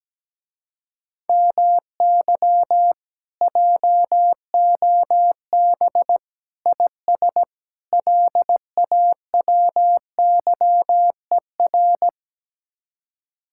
Morse code, heard as a keyed tone, about 17 wpm, 700 Hz. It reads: MY JOB IS LAWYER
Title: Speed-Racing Common QSO Phrases - 17wpm